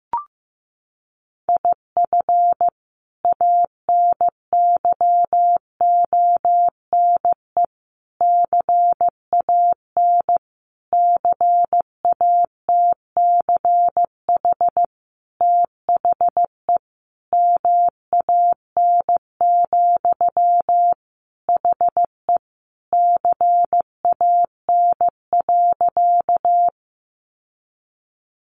Morse code, heard as a keyed tone, about 15 wpm, 700 Hz. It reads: IF ANYONE CAN CATCH THE MAN, HE CAN.
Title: Sentences from Top 1500 Words 15wpm